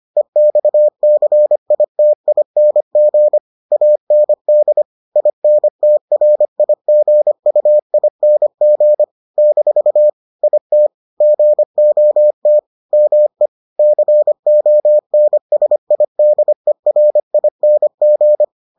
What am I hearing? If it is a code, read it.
Morse code, heard as a keyed tone, about 25 wpm, 600 Hz. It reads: EXCITING AND INTRIGUING - IT GOT ME CONSIDERING